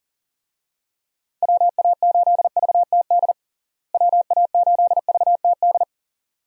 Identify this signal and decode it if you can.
Morse code, heard as a keyed tone, about 40 wpm, 700 Hz. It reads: WA8VTD WA8VTD
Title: USA Call Signs - 6 Characters - Repeated 2x - 40wpm